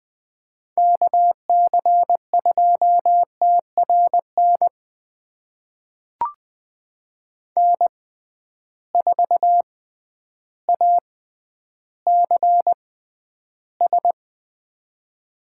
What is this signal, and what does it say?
Morse code, heard as a keyed tone, about 20 wpm, spaced as for 11 wpm, 700 Hz. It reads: KC2TRN N4ACS
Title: USA Call Signs - 6x Character Spacing - 20wpm